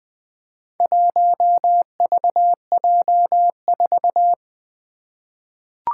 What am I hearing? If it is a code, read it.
Morse code, heard as a keyed tone, about 20 wpm, 700 Hz. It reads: 1VJ4
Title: Four Letter-Number Combinations - 20wpm